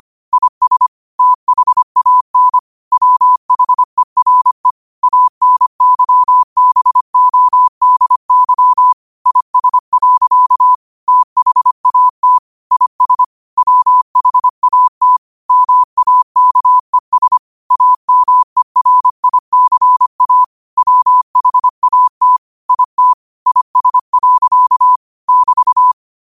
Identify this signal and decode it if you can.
Morse code, heard as a keyed tone, about 25 wpm, 1000 Hz. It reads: IS THAN WHERE ANYBODY IS. THAT IS WHAT MAKES AMERICA WHAT IT IS. <BT>